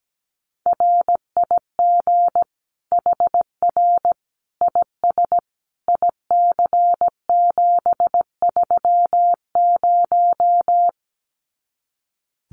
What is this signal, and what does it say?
Morse code, heard as a keyed tone, about 17 wpm, 700 Hz. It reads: RIG HR IS IC730
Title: Common QSO Phrases 17wpm